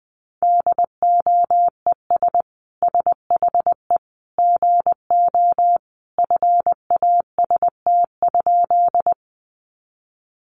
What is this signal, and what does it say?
Morse code, heard as a keyed tone, about 20 wpm, 700 Hz. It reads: DOES SHE GO FAST?
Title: Sentences from Top 400 Words 20wpm